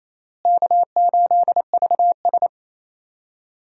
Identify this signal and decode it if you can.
Morse code, heard as a keyed tone, about 27 wpm, 700 Hz. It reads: K8VS
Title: USA Call Signs - 4 Characters - Repeated 3x - 28wpm